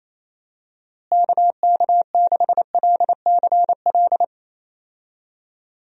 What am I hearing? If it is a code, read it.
Morse code, heard as a keyed tone, about 27 wpm, 700 Hz. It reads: KK6LCL
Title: USA Call Signs - 6 Characters - Repeated 2x - 28wpm